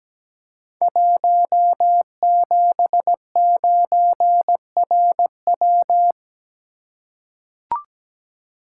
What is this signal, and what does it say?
Morse code, heard as a keyed tone, about 17 wpm, 700 Hz. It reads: 179RW
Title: Code Groups - 5 Letter:Numbers - 17wpm